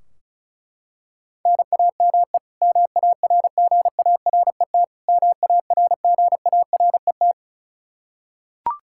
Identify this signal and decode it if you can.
Morse code, heard as a keyed tone, about 36 wpm, 700 Hz. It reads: NAME MARGARET MARGARET
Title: Practice QSO - Exchange 1 - Beginning - 35wpm